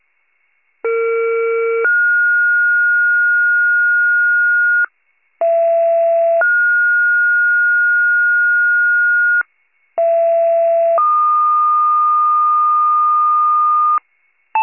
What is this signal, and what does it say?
Two-tone page, heard as a one-second tone, then a three-second tone, about 460 then 1430 Hz, about 670 then 1430 Hz, about 670 then 1220 Hz, with noise. It is heard from 0.8 to 14.0 s.
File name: pager / 39 FIRE / wcalert